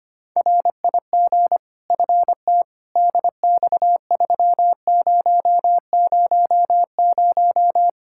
Morse code, heard as a keyed tone, about 25 wpm, 700 Hz. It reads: RIG FT DX3000
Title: Practice QSO - Exchange 2 - Info - 25wpm